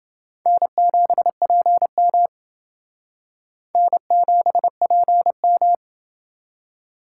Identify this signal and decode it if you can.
Morse code, heard as a keyed tone, about 28 wpm, 700 Hz. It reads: N7PM N7PM